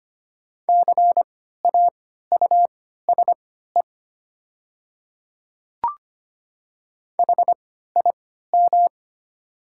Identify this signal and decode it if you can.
Morse code, heard as a keyed tone, about 25 wpm, spaced as for 14 wpm, 700 Hz. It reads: CAUSE HIM
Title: 3x Character Spacing - Top 200 Words - 25wpm